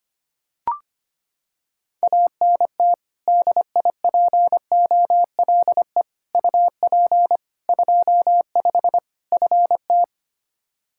Morse code, heard as a keyed tone, about 25 wpm, 700 Hz. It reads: ANT DIPOLE UP 25 FT
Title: Practice QSO - Exchange 2 - Info - 25wpm